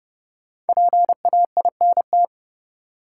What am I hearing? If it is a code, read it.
Morse code, heard as a keyed tone, about 30 wpm, 700 Hz. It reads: PAINT
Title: Sentences from Top 1000 Words - Review Part 6 - New Word Drill - 30wpm